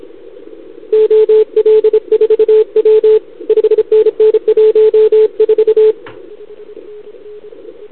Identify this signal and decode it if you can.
Morse code, heard as a keyed tone, about 29 wpm, 440 Hz, with radio noise. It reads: OL4W 5NN 14
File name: CQWWCW2021 / Doderna OL3Z